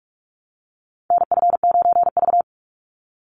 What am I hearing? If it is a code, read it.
Morse code, heard as a keyed tone, about 44 wpm, 700 Hz. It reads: NF9V
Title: Call Signs - 45wpm